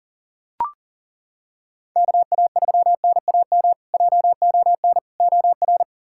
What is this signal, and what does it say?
Morse code, heard as a keyed tone, about 37 wpm, 700 Hz. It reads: KA3NAM JON OR
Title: K1USN SST QSO Practice - 40wpm